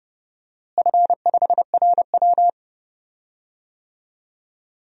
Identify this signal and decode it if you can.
Morse code, heard as a keyed tone, about 31 wpm, 700 Hz. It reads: F5RW
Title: DXCC-100 Call Signs - 4 Characters - Repeated 3x - 30wpm